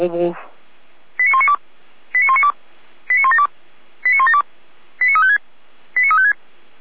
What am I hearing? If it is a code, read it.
Five-tone selective call (ZVEI), heard as quick runs of five tones, tones of about 70 ms, with speech. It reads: 89192 89192 89172 89172 89356 89356